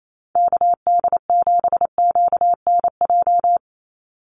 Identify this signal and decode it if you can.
Morse code, heard as a keyed tone, about 28 wpm, 700 Hz. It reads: KD7QNJ
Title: USA Call Signs 28wpm